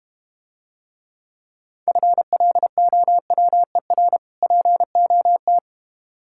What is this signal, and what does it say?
Morse code, heard as a keyed tone, about 32 wpm, 700 Hz. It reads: FLOWER POT